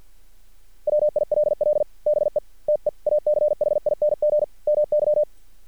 Morse code, heard as a keyed tone, about 50 wpm, 600 Hz. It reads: WILL BE TEACHING MY
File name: archived / CRSnet